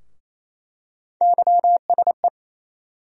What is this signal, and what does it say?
Morse code, heard as a keyed tone, about 28 wpm, 700 Hz. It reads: YSE
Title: USA Call Sign Postfixes - 3 Letters 28wpm